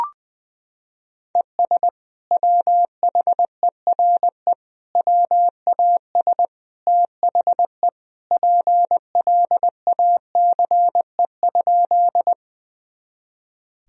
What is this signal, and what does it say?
Morse code, heard as a keyed tone, about 20 wpm, 700 Hz. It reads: ES WHERE WAS THE PLACE?
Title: Sentences from Top 200 Words and Abbreviations 20wpm